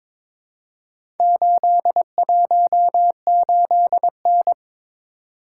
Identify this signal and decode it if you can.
Morse code, heard as a keyed tone, about 22 wpm, 700 Hz. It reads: 818N
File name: practice / 22wpm / Four Letter-Number Combinations - 22wpm